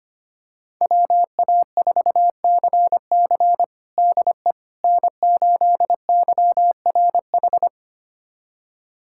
Morse code, heard as a keyed tone, about 25 wpm, 700 Hz. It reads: WA4CC DE N8YRH
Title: USA Callsign DE Callsign - 5 Characters - 25wpm